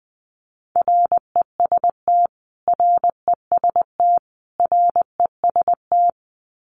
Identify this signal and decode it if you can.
Morse code, heard as a keyed tone, about 20 wpm, 700 Hz. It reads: REST REST REST
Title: Top 500 Words Repeated 20wpm